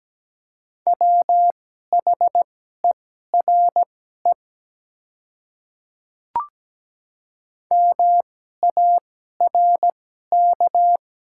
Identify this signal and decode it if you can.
Morse code, heard as a keyed tone, about 17 wpm, spaced as for 12 wpm, 700 Hz. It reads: WHERE MARK
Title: 2x Character Spacing - Top 300 Words - 17wpm